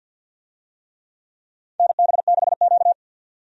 Morse code, heard as a keyed tone, about 51 wpm, 700 Hz. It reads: NB6Q
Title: USA Call Signs - 4 Characters - 50wpm